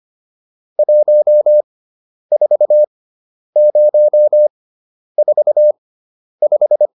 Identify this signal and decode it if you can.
Morse code, heard as a keyed tone, about 25 wpm, spaced as for 10 wpm, 600 Hz. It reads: 14045